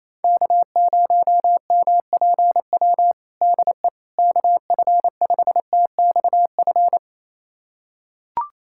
Morse code, heard as a keyed tone, about 27 wpm, 700 Hz. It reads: K0MPW DE KF5TXF
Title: USA Callsign DE Callsign 28wpm